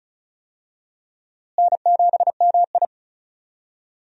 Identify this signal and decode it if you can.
Morse code, heard as a keyed tone, about 34 wpm, 700 Hz. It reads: N7MI
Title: USA Call Signs - 4 Characters - Repeated 2x - 35wpm